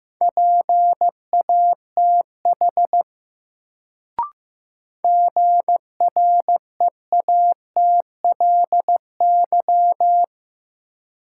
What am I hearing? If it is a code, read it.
Morse code, heard as a keyed tone, about 15 wpm, 700 Hz. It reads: PATH GREATLY
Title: Sentences from Top 1400 Words - New Word Drill - 15wpm